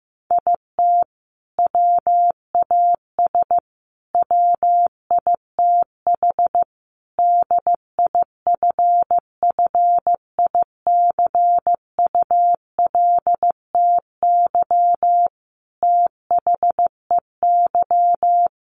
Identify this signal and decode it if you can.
Morse code, heard as a keyed tone, about 15 wpm, 700 Hz. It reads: IT WAS WITH DIFFICULTY THEY
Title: Sentences from Top 1500 Words 15wpm